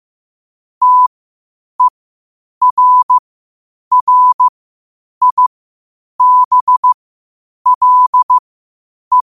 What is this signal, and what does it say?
Morse code, heard as a keyed tone, about 15 wpm, spaced as for 8 wpm, 1000 Hz. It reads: TERRIBLE
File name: qotdfiles / 8wpm